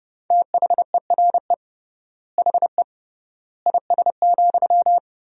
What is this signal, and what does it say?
Morse code, heard as a keyed tone, about 30 wpm, 700 Hz. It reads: THERE HE IS,
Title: 3x Word-Spacing - Sentences from Top 100 Words - 30wpm